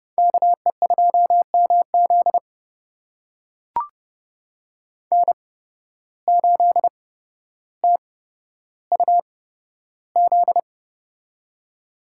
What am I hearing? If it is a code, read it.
Morse code, heard as a keyed tone, about 29 wpm, spaced as for 13 wpm, 700 Hz. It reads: KE2MZ N8TUZ
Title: USA Call Signs - 5 Characters - 8x Character Spacing - 30wpm